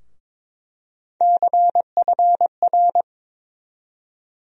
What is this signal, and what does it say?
Morse code, heard as a keyed tone, about 22 wpm, 700 Hz. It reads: CFR
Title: USA Call Sign Postfixes - 3 Letters 22wpm